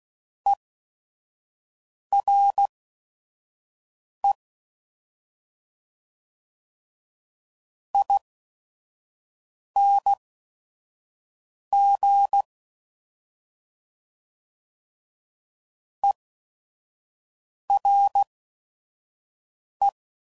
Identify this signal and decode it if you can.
Morse code, heard as a keyed tone, about 16 wpm, spaced as for 5 wpm, 800 Hz. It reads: ERE ING ERE
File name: Suffixes / General / ERE INGs